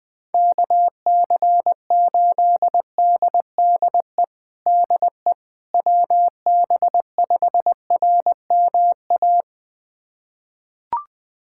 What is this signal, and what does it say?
Morse code, heard as a keyed tone, about 20 wpm, 700 Hz. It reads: KC8DDE DE WB5RMA
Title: USA Callsign DE Callsign - 6 Characters - 20wpm